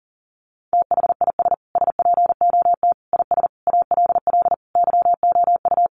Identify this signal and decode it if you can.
Morse code, heard as a keyed tone, about 40 wpm, 700 Hz. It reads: THIS SPOT IS ALL YOU
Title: Sentences from Top 1100 Words 40wpm